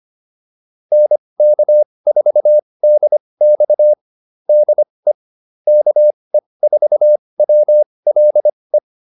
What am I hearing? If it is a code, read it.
Morse code, heard as a keyed tone, about 25 wpm, spaced as for 20 wpm, 600 Hz. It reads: NK4DX DE KE4WLE